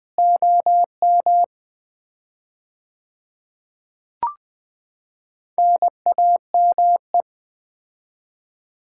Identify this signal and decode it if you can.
Morse code, heard as a keyed tone, about 20 wpm, 700 Hz. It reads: OM NAME
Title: Common QSO Phrases - 4x Word-Spacing - 20wpm